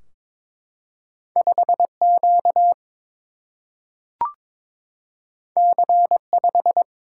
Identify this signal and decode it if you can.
Morse code, heard as a keyed tone, about 22 wpm, 700 Hz. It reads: 5 Q C 5
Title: Single-Letter Single-Number 22wpm